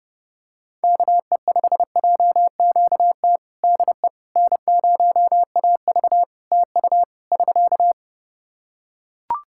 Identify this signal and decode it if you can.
Morse code, heard as a keyed tone, about 32 wpm, 700 Hz. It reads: KE5JQT DE N0AV TU <SK>
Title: Practice QSO - Exchange 4 - Ending - 30wpm